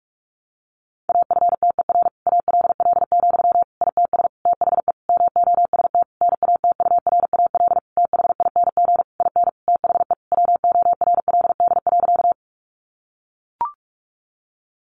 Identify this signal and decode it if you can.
Morse code, heard as a keyed tone, about 44 wpm, 700 Hz. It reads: AFTER ALL, ITS THE MOST NATURAL THING IN THE WORLD.
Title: Sentences from Top 700 Words 45wpm